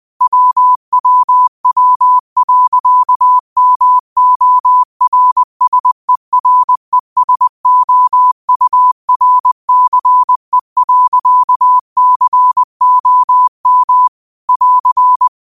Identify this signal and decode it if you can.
Morse code, heard as a keyed tone, about 20 wpm, 1000 Hz. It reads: WWW.MORSERESOURCE.COM <AR>
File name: qotdfiles / 20wpm